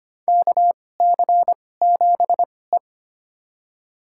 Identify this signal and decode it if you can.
Morse code, heard as a keyed tone, about 25 wpm, spaced as for 18 wpm, 700 Hz. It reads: KC7E
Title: USA Call Signs - 4 Characters - 2x Character Spacing - 25wpm